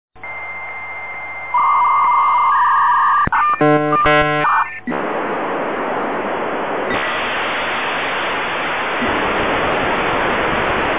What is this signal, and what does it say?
Dial-up modem handshake.
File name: dialup